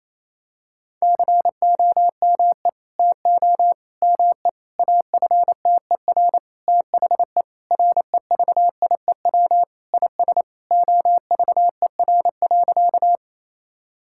Morse code, heard as a keyed tone, about 28 wpm, 700 Hz. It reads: COME TO ME AFTER THE REVIEW IS OVER.
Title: Sentences from Top 1000 Words - Review Part 6 - 28wpm